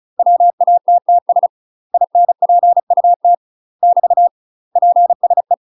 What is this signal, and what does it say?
Morse code, heard as a keyed tone, about 34 wpm, 700 Hz. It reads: WATTS INPUT <BT> PSE
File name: UK MASTER QSOs TRAINING / 35WPM / UK MASTER QSOs 0024